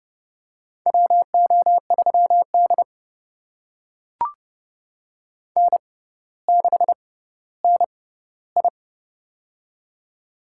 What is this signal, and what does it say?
Morse code, heard as a keyed tone, about 30 wpm, spaced as for 15 wpm, 700 Hz. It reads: WO3D N6NI
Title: USA Call Signs - 6x Character Spacing - 30wpm